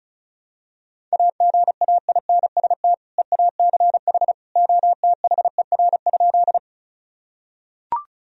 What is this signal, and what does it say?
Morse code, heard as a keyed tone, about 34 wpm, 700 Hz. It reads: AGAINST EACH OTHER?
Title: Sentences from Top 1000 Words - Review Part 1 - 35wpm